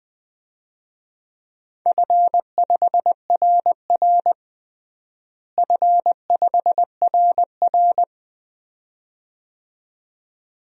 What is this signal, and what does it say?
Morse code, heard as a keyed tone, about 20 wpm, 700 Hz. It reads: F5RR F5RR
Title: DXCC-100 Call Signs - 4 Characters - Repeated 2x - 20wpm